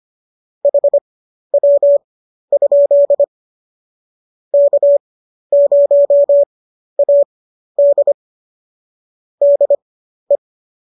Morse code, heard as a keyed tone, about 25 wpm, spaced as for 12 wpm, 570 Hz. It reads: HW? K0AD DE